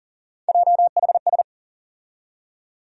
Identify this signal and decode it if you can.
Morse code, heard as a keyed tone, about 40 wpm, 700 Hz. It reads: JHS
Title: USA Call Sign Postfixes - 3 Letters 40wpm